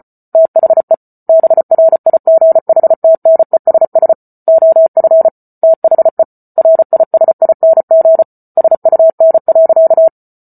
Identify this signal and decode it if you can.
Morse code, heard as a keyed tone, about 34 wpm, 650 Hz. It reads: THE BRIGHTNESS OF THE RISING SUN.